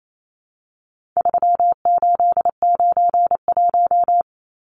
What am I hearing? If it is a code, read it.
Morse code, heard as a keyed tone, about 28 wpm, 700 Hz. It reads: 3891
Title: Numbers - 4-Digit 28wpm